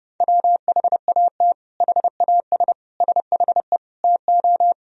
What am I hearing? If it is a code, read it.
Morse code, heard as a keyed tone, about 32 wpm, 700 Hz. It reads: WHAT HAS SHE TO
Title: Sentences from Top 100 Words and Abbreviations 30wpm